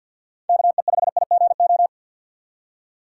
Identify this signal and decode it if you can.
Morse code, heard as a keyed tone, about 50 wpm, 700 Hz. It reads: KE5IGO
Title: USA Call Signs - 2x Character Spacing - 50wpm